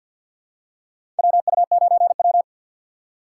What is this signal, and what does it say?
Morse code, heard as a keyed tone, about 52 wpm, 700 Hz. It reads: WU9W
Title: USA Call Signs - 4 Characters - 2x Character Spacing - 50wpm